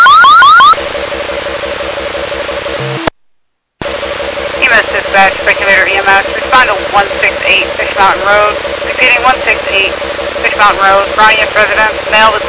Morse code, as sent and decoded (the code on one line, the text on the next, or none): none